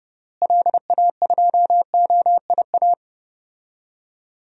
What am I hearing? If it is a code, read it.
Morse code, heard as a keyed tone, about 30 wpm, 700 Hz. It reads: LA2OIA
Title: DXCC-100 Call Signs - 6 Characters - Repeated 2x - 30wpm